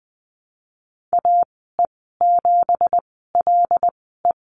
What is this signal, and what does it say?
Morse code, heard as a keyed tone, about 19 wpm, spaced as for 14 wpm, 700 Hz. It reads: AE7LE
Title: 179 Lesson 29g - Slash - Call Signs 20wpm